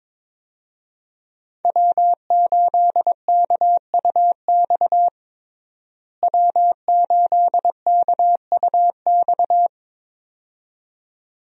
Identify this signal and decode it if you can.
Morse code, heard as a keyed tone, about 22 wpm, 700 Hz. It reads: W8KUX W8KUX
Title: USA Call Signs - 5 Characters - Repeated 2x - 22wpm